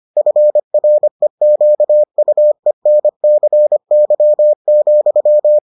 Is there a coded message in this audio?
Yes: Morse code from 0.2 to 5.6 s, heard as a keyed tone, about 25 wpm, 600 Hz. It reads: FREQUENCY,